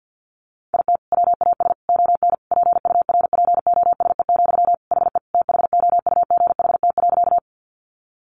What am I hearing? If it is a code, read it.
Morse code, heard as a keyed tone, about 50 wpm, 700 Hz. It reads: IT WAS ON PURPOSE, HE THOUGHT.
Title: Sentences from Top 800 Words 50wpm